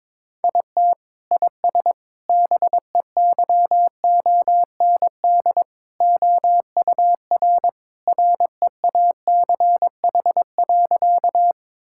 Morse code, heard as a keyed tone, about 22 wpm, 700 Hz. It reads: IT IS BEYOND OUR REACH.
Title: Sentences from Top 1000 Words - Review Part 6 - 22wpm